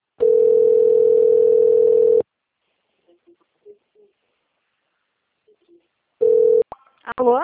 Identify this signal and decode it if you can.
Ringback tone (North America), 440 + 480 Hz, with speech.
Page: Jacarei - SP